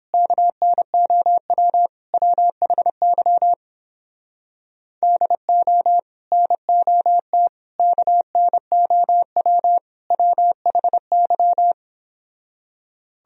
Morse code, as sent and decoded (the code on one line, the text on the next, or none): -.- -. --- .-- / .-- .... -.-- / -.. --- / -. --- - / -.- -. --- .-- / .-- .... -.--
KNOW WHY DO NOT KNOW WHY